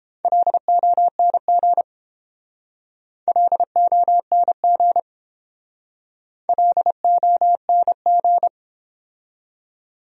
Morse code, heard as a keyed tone, about 29 wpm, 700 Hz. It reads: LONG LONG LONG